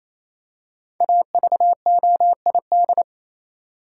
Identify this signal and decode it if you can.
Morse code, heard as a keyed tone, about 28 wpm, 700 Hz. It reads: AVOID
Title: Five Letter Words - 28wpm